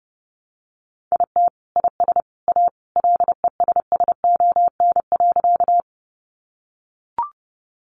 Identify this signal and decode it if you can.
Morse code, heard as a keyed tone, about 30 wpm, 700 Hz. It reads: IT IS A LESSON.
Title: Sentences from Top 1200 Words 30wpm